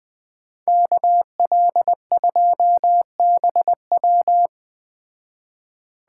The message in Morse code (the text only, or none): KL2BW